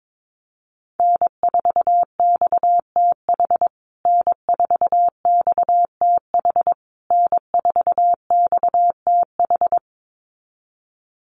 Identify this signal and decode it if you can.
Morse code, heard as a keyed tone, about 22 wpm, 700 Hz. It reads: N4XTH N4XTH N4XTH
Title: USA Call Signs Repeated 3x 22wpm